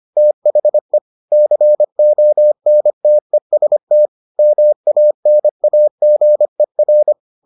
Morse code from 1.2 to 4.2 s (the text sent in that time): CONTEST